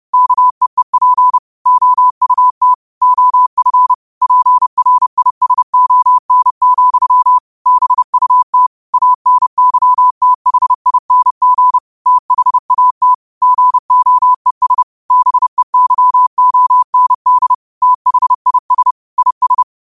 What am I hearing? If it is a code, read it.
Morse code, heard as a keyed tone, about 30 wpm, 1000 Hz. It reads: KEEP OUT OF PRISON, BUT ANYTHING THAT GOES BEYOND THIS IS
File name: qotdfiles / 30wpm